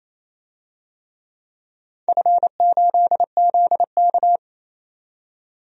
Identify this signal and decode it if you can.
Morse code, heard as a keyed tone, about 28 wpm, 700 Hz. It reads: F8ZK
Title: DXCC-100 Call Signs - 4 Characters - 28wpm